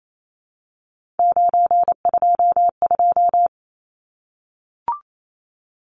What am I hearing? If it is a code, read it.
Morse code, heard as a keyed tone, about 28 wpm, 700 Hz. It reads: 922
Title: Numbers - 3-Digit 28wpm